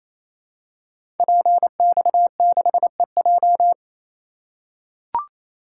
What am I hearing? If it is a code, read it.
Morse code, heard as a keyed tone, about 27 wpm, 700 Hz. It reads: PX6EJ
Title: Code Groups - 5 Letter:Numbers - 28wpm